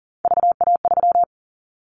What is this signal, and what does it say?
Morse code, heard as a keyed tone, about 40 wpm, 700 Hz. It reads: VA3
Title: Callsign Prefixes for Top 50 DXCC - 40wpm